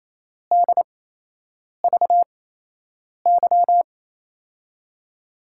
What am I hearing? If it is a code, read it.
Morse code, heard as a keyed tone, about 28 wpm, spaced as for 8 wpm, 700 Hz. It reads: DVY
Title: USA Call Signs - 8x Character Spacing - 28wpm